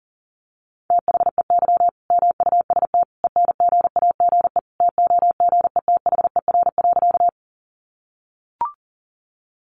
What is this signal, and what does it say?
Morse code, heard as a keyed tone, about 40 wpm, 700 Hz. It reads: THEY MUST ENGAGE TOGETHER.
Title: Sentences from Top 1500 Words 40wpm